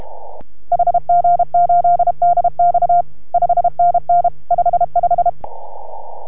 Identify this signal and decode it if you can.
Morse code, heard as a keyed tone, about 33 wpm, 690 Hz. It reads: HG8DX 5NN 55